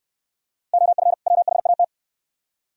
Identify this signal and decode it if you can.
Morse code, heard as a keyed tone, about 69 wpm, 690 Hz. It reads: OF WHAT